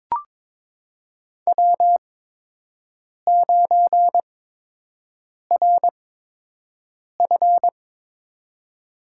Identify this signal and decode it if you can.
Morse code, heard as a keyed tone, about 22 wpm, spaced as for 6 wpm, 700 Hz. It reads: W9RF
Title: USA Call Signs - 4 Characters - 8x Character Spacing - 22wpm